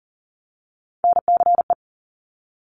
Morse code, heard as a keyed tone, about 41 wpm, 690 Hz. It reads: NCE
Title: Top Three Letter Combinations in Words 40wpm